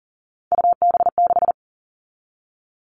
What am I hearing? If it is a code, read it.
Morse code, heard as a keyed tone, about 40 wpm, 700 Hz. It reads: UB6
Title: All Cut Numbers - 3-Digit 40wpm